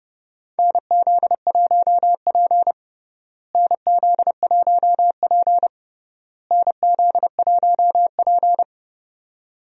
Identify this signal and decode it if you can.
Morse code, heard as a keyed tone, about 30 wpm, 700 Hz. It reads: NZ1P NZ1P NZ1P
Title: USA Call Signs - 4 Characters - Repeated 3x - 30wpm